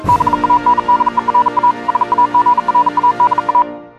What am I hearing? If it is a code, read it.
Morse code, heard as a keyed tone, about 42 wpm, 980 Hz. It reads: BANDERA SAGRADA